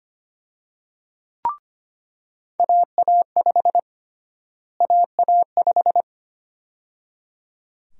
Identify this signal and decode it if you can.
Morse code, heard as a keyed tone, about 25 wpm, 700 Hz. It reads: AA5 AA5
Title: USA Call Sign Prefixes - Repeated 2x - 25wpm